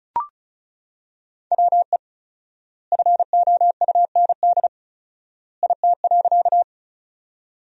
Morse code, heard as a keyed tone, about 36 wpm, 700 Hz. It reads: WE FOUND IT.
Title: 4x Word-Spacing - Sentences from Top 300 Words - 35wpm